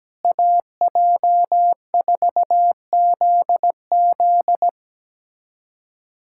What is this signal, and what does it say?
Morse code, heard as a keyed tone, about 17 wpm, 700 Hz. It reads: AJ4ZZ
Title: USA Callsign DE Callsign - 5 Characters - 17wpm